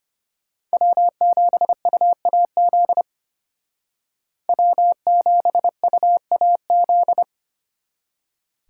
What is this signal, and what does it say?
Morse code, heard as a keyed tone, about 33 wpm, 700 Hz. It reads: W7UAZ W7UAZ